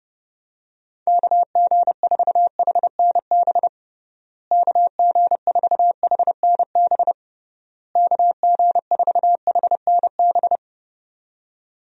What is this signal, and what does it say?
Morse code, heard as a keyed tone, about 29 wpm, 700 Hz. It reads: KG4HNB KG4HNB KG4HNB
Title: USA Call Signs - 6 Characters - Repeated 3x - 30wpm